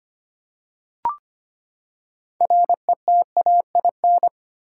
Morse code, heard as a keyed tone, about 25 wpm, 700 Hz. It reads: RETAIN
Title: Six Letter Words - 25wpm